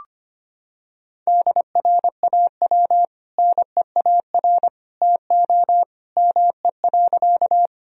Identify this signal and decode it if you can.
Morse code, heard as a keyed tone, about 25 wpm, 700 Hz. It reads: DRAW NEAR TO ME.
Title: Sentences from Top 1000 Words - Review Part 1 - 25wpm